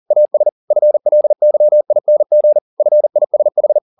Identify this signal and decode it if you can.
Morse code, heard as a keyed tone, about 40 wpm, 600 Hz. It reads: AS FLYING FISH